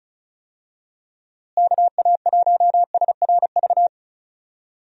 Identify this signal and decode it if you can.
Morse code, heard as a keyed tone, about 36 wpm, 700 Hz. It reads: KA1SRV
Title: USA Call Signs - 6 Characters - Repeated 3x - 35wpm